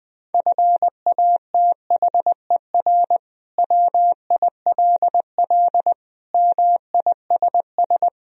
Morse code, heard as a keyed tone, about 20 wpm, 700 Hz. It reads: FATHER WILL MISS
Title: Sentences from Top 1000 Words - Review Part 3 - 20wpm